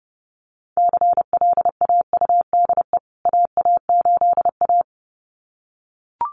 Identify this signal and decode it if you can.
Morse code, heard as a keyed tone, about 28 wpm, 700 Hz. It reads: CLAUDE AA8A
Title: CWT Exchange 30wpm